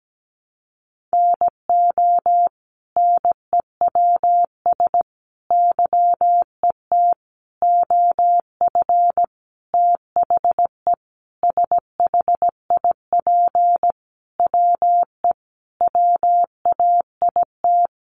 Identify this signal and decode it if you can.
Morse code, heard as a keyed tone, about 17 wpm, 700 Hz. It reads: NO NEWS YET OF THE SHIP WE WAIT
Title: Sentences from Top 800 Words 17wpm